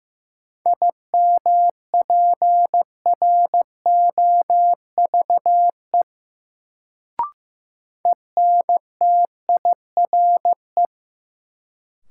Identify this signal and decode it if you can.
Morse code, heard as a keyed tone, about 16 wpm, 700 Hz. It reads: IMPROVE ENTIRE
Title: Sentences from Top 800 Words - New Word Drill - 15wpm